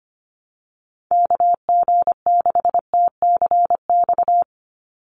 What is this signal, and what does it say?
Morse code, heard as a keyed tone, about 25 wpm, 700 Hz. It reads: KG6TCX
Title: USA Call Signs 25wpm